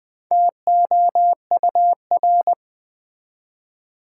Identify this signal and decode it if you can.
Morse code, heard as a keyed tone, about 20 wpm, 700 Hz. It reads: TOUR
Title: Build A Word - 2 Letters At A Time - 4-Letter Words - 20wpm